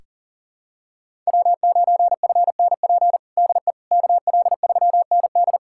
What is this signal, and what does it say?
Morse code, heard as a keyed tone, about 40 wpm, 700 Hz. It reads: W9FNP DE KL3ND
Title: USA Callsign DE Callsign - 5 Characters - 40wpm